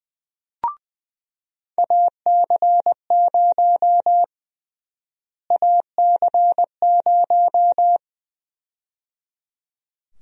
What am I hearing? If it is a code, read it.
Morse code, heard as a keyed tone, about 20 wpm, 700 Hz. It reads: AC0 AC0
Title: USA Call Sign Prefixes - Repeated 2x - 20wpm